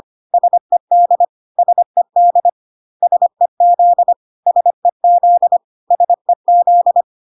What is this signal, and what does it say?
Morse code, heard as a keyed tone, about 25 wpm, 700 Hz. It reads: SED SED SEZ SEZ SEZ
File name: CW abbreviations alphabetical repeated 3 times 25 25wpm